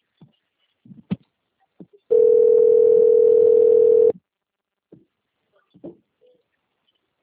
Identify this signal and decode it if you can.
Ringback tone (North America), 440 + 480 Hz.